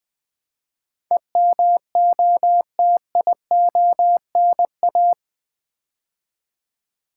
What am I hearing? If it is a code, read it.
Morse code, heard as a keyed tone, about 20 wpm, 700 Hz. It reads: EMOTIONA
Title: Build A Word - 4 Letters At A Time - 11-Letter Words - 20wpm